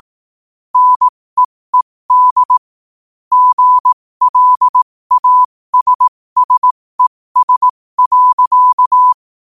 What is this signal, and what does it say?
Morse code, heard as a keyed tone, about 18 wpm, spaced as for 15 wpm, 1000 Hz. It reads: NEED GLASSES.
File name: qotdfiles / 15wpm